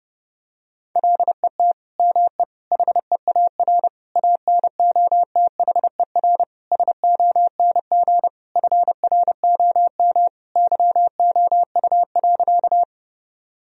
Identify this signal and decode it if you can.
Morse code, heard as a keyed tone, about 32 wpm, 700 Hz. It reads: LET ME HEAR ANOTHER SONG FROM YOU.
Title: Sentences from Top 1000 Words - Review Part 2 - 30wpm